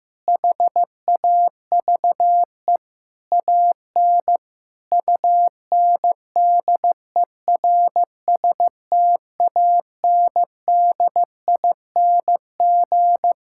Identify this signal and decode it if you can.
Morse code, heard as a keyed tone, about 15 wpm, 700 Hz. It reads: HAVE AN UNDERSTANDING
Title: Sentences from Top 1000 Words - Review Part 5 - 15wpm